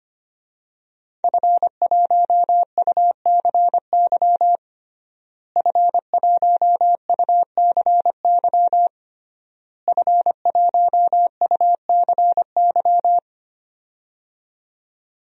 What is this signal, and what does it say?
Morse code, heard as a keyed tone, about 25 wpm, 700 Hz. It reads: F1UCY F1UCY F1UCY
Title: DXCC-100 Call Signs - 5 Characters - Repeated 3x - 25wpm